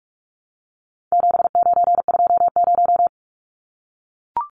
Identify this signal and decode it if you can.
Morse code, heard as a keyed tone, about 44 wpm, 700 Hz. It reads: 7920
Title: Numbers - 4-Digit 45wpm